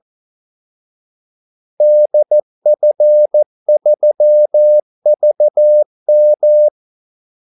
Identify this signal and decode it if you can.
Morse code, heard as a keyed tone, about 14 wpm, 600 Hz. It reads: DF3VM
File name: Callsigns-14wpm